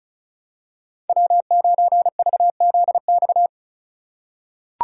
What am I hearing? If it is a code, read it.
Morse code, heard as a keyed tone, about 36 wpm, 700 Hz. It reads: W9VZX
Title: USA Call Signs - 5 Characters - 2x Character Spacing - 35wpm